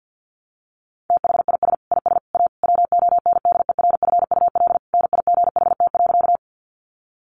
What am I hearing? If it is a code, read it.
Morse code, heard as a keyed tone, about 50 wpm, 700 Hz. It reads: THIS IS A WONDERFUL NIGHT.
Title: Sentences from Top 1100 Words 50wpm